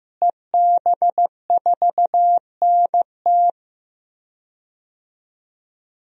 Morse code, heard as a keyed tone, about 16 wpm, 700 Hz. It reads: EB4NT
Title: DXCC-100 Call Signs - 5 Characters - Repeated 2x - 15wpm